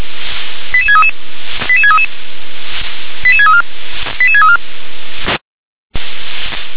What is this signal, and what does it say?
Five-tone selective call (ZVEI), heard as quick runs of five tones, tones of about 80 ms. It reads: 88633 88633 88634 88634
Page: ZVEI